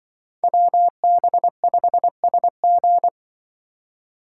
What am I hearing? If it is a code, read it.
Morse code, heard as a keyed tone, about 24 wpm, 700 Hz. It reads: WB5SG
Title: Speed Racing - USA 2x2 Call Signs 20wpm